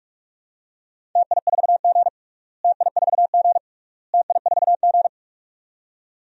Morse code, heard as a keyed tone, about 44 wpm, 700 Hz. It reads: TI4G TI4G TI4G
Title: DXCC-100 Call Signs - 4 Characters - Repeated 3x - 45wpm